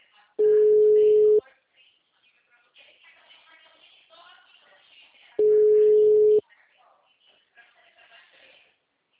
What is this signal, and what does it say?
Ringback tone (Europe, 425 Hz), with speech.
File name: out- -2014- - - .7129